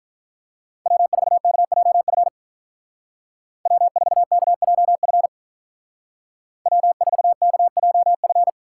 Morse code, heard as a keyed tone, about 45 wpm, 700 Hz. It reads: W4KJF W4KJF W4KJF